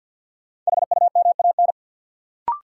Morse code, heard as a keyed tone, about 51 wpm, 700 Hz. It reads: HUMAN
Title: Sets of 5 Words - New Word Drill - 50wpm